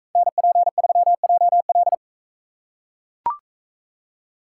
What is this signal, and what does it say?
Morse code, heard as a keyed tone, about 41 wpm, 700 Hz. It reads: NP3JL